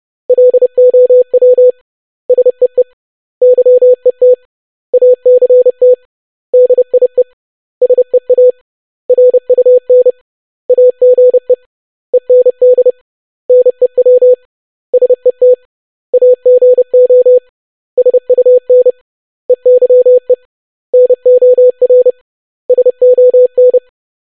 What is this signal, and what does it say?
Morse code, heard as a keyed tone, about 30 wpm, 500 Hz. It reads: LOW SEE YET ACT DIE SEA RUN AGE END NEW SET AGO SUN EYE NOR SON
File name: basic500-25wpm